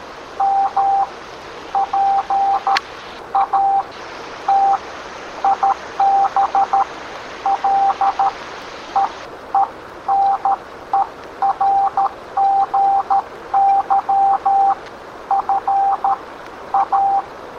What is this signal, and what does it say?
Morse code, heard as a keyed tone, about 13 wpm, spaced as for 10 wpm, 790 Hz, with noise. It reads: MPATIBLE ENERGY FA